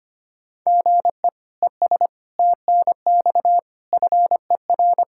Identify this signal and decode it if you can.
Morse code, heard as a keyed tone, about 25 wpm, 700 Hz. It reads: GE ES TNX FER
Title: K1USN SST QSO Practice 25wpm